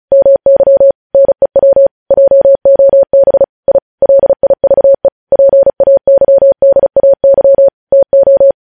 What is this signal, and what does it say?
Morse code, heard as a keyed tone, about 36 wpm, 550 Hz. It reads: MY NEW JOB I LIVE PAYDAY TO
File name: podcast / News headlines-1x-35x35